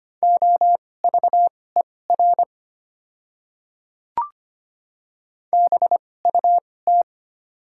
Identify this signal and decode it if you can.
Morse code, heard as a keyed tone, about 25 wpm, spaced as for 18 wpm, 690 Hz. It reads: OVER BUT
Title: 2x Character Spacing - Top 500 Words - 25wpm